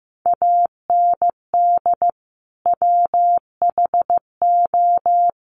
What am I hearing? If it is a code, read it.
Morse code, heard as a keyed tone, about 15 wpm, 700 Hz. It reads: AND WHO
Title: Sentences from Top 200 Words 15wpm